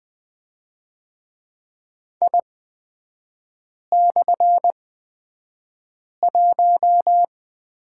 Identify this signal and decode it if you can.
Morse code, heard as a keyed tone, about 20 wpm, spaced as for 5 wpm, 700 Hz. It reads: I/1
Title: 224 Lesson 36e - BT - Three Characters 20wpm